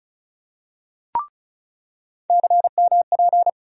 Morse code, heard as a keyed tone, about 36 wpm, 700 Hz. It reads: CMP
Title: USA Call Sign Postfixes - 3 Letters 35wpm